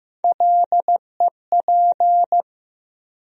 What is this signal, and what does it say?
Morse code, heard as a keyed tone, about 15 wpm, 700 Hz. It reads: LEP
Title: USA Call Sign Postfixes - 3 Letters 15wpm